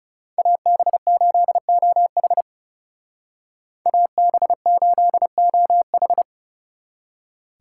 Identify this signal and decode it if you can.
Morse code, heard as a keyed tone, about 39 wpm, 700 Hz. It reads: AB8OH AB8OH